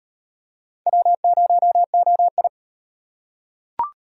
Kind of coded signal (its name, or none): Morse code